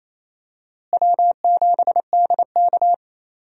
Morse code, heard as a keyed tone, about 28 wpm, 700 Hz. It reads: W7DK
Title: USA Call Signs - 4 Characters - Repeated 2x - 28wpm